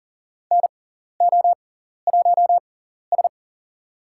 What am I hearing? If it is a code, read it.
Morse code, heard as a keyed tone, about 40 wpm, spaced as for 14 wpm, 700 Hz. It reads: NO1S
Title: USA Call Signs - 4 Characters - 6x Character Spacing - 40wpm